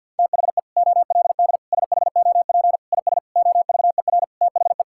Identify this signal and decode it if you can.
Morse code, heard as a keyed tone, about 50 wpm, 700 Hz. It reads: THE OLD SHOP IS OVER THE
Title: Sentences from Top 1600 Words 50wpm